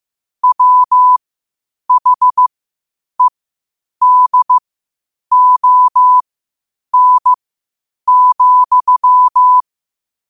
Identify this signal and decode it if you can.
Morse code, heard as a keyed tone, about 15 wpm, spaced as for 8 wpm, 1000 Hz. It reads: WHEDON,
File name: qotdfiles / 8wpm